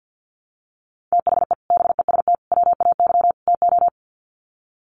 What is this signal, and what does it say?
Morse code, heard as a keyed tone, about 50 wpm, 700 Hz. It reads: THE BEST WAY TO
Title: Sets of 4 Words 50wpm